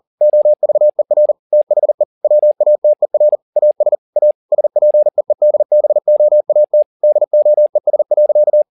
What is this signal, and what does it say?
Morse code, heard as a keyed tone, about 40 wpm, 600 Hz. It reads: OVER THE WATER AS A SPEEDBOAT DOES.
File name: the old man and the sea 2 40wpm